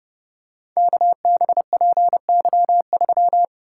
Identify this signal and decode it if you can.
Morse code, heard as a keyed tone, about 31 wpm, 700 Hz. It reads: KBPY3
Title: Code Groups - 5 Letter:Numbers - 30wpm